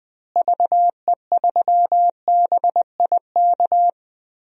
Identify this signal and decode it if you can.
Morse code, heard as a keyed tone, about 20 wpm, 700 Hz. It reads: VE3BIK
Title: DXCC-100 Call Signs - 6 Characters - Repeated 3x - 20wpm